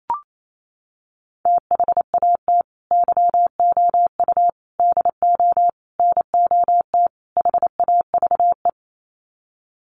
Morse code, heard as a keyed tone, about 28 wpm, 700 Hz. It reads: THAT YOU DO NOT HAVE
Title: Sets of 5 Words Encore 28wpm